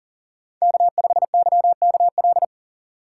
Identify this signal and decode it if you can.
Morse code, heard as a keyed tone, about 40 wpm, 700 Hz. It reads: K5YKL
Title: USA Call Signs - 5 Characters - 6x Character Spacing - 40wpm